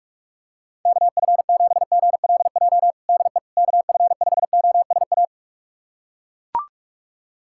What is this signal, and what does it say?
Morse code, heard as a keyed tone, about 42 wpm, 690 Hz. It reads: KF7GLJ DE KF5OSA
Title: USA Callsign DE Callsign - 6 Characters - 45wpm